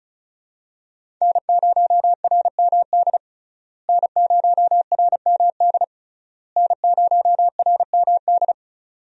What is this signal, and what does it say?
Morse code, heard as a keyed tone, about 36 wpm, 700 Hz. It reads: N0RMD N0RMD N0RMD
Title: USA Call Signs - 5 Characters - Repeated 3x - 35wpm